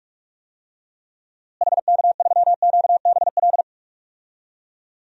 Morse code, heard as a keyed tone, about 44 wpm, 700 Hz. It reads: HK3QBL
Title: DXCC-100 Call Signs - 6 Characters - Repeated 2x - 45wpm